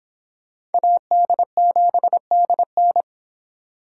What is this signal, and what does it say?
Morse code, heard as a keyed tone, about 30 wpm, 700 Hz. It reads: AD7DN